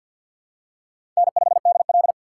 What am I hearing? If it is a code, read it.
Morse code, heard as a keyed tone, about 47 wpm, 700 Hz. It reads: N5DL
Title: USA Call Signs - 4x Character Spacing - 50wpm